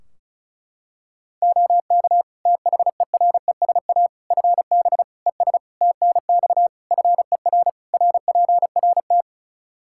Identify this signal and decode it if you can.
Morse code, heard as a keyed tone, about 34 wpm, 700 Hz. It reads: OK THERESA FB ES TNX FER RPRT
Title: Practice QSO - Exchange 2 - Info - 35wpm